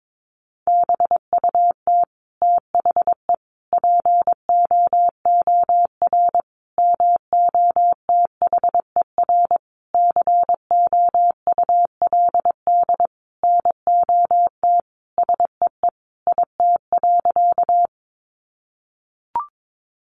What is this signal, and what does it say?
Morse code, heard as a keyed tone, about 22 wpm, 700 Hz. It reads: BUT THE POOR MOTHER COULD NOT SEE IT.
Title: Sentences from Top 1000 Words 22wpm